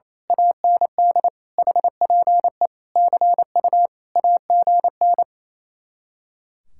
Morse code, heard as a keyed tone, about 26 wpm, 700 Hz. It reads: AND HPE CU AGN
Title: Speed-Racing Common QSO Phrases - 28wpm